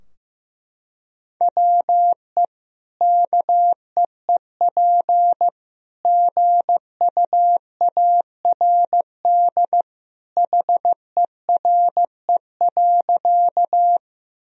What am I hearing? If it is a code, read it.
Morse code, heard as a keyed tone, about 15 wpm, 700 Hz. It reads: WE KEEP GUARD HERE.
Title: Sentences from Top 1000 Words - Review Part 4 - 15wpm